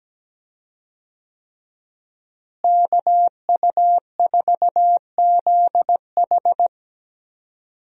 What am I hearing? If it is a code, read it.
Morse code, heard as a keyed tone, about 17 wpm, 700 Hz. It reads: KU4ZH
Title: USA Call Signs - 5 Characters - Repeated 3x - 17wpm